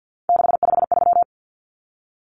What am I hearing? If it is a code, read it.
Morse code, heard as a keyed tone, about 50 wpm, 700 Hz. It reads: 653
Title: Numbers - 3-Digit 50wpm